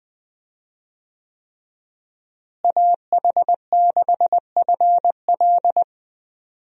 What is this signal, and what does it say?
Morse code, heard as a keyed tone, about 20 wpm, 700 Hz. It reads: AH6FL
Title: USA Call Signs - 5 Characters - Repeated 3x - 20wpm